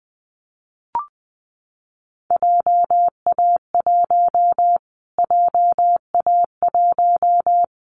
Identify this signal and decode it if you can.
Morse code, heard as a keyed tone, about 20 wpm, 700 Hz. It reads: JA1 JA1
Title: Callsign Prefixes for Top 25 DXCC - Repeated 3x - 20wpm